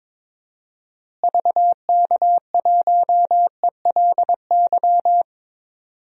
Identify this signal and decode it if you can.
Morse code, heard as a keyed tone, about 22 wpm, 700 Hz. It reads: VK1ELY
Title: DXCC-100 Call Signs - 6 Characters - Repeated 3x - 22wpm